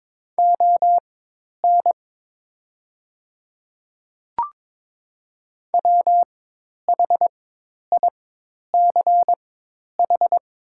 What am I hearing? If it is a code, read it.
Morse code, heard as a keyed tone, about 23 wpm, spaced as for 10 wpm, 700 Hz. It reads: ON WHICH
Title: 4x Character Spacing - Top 100 Words - 22wpm